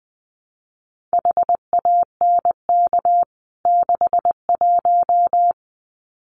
Morse code, heard as a keyed tone, about 20 wpm, 700 Hz. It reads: HANK 61
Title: CWT Exchange 20wpm